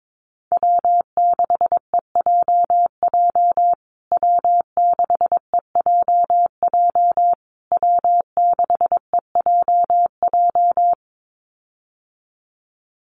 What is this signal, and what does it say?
Morse code, heard as a keyed tone, about 22 wpm, 700 Hz. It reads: W6EJJ W6EJJ W6EJJ
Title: USA Call Signs Repeated 3x 22wpm